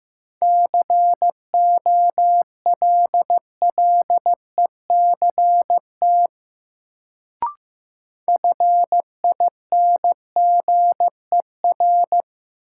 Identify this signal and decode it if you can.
Morse code, heard as a keyed tone, about 16 wpm, 700 Hz. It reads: COLLECT FINGER
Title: Sentences from Top 1600 Words - New Word Drill Alt - 15wpm